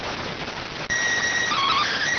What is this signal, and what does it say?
SSTV (slow-scan television), PD 290.